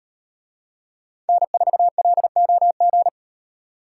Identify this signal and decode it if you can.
Morse code, heard as a keyed tone, about 38 wpm, 700 Hz. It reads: N4LOG